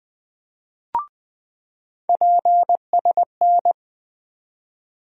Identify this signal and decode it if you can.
Morse code, heard as a keyed tone, about 21 wpm, 700 Hz. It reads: PSN
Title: USA Call Sign Postfixes - 3 Letters 20wpm